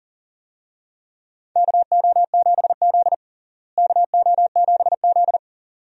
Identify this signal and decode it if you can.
Morse code, heard as a keyed tone, about 40 wpm, 700 Hz. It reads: KO7Z KO7Z
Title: USA Call Signs - 4 Characters - Repeated 2x - 40wpm